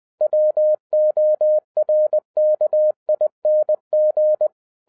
Morse code, heard as a keyed tone, about 20 wpm, 610 Hz. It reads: WORKING